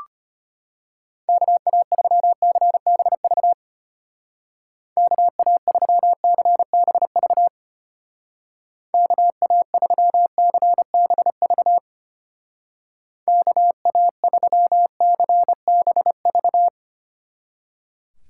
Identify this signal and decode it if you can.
Morse code, heard as a keyed tone, about 33 wpm, 700 Hz. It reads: KA3CBV KA3CBV KA3CBV KA3CBV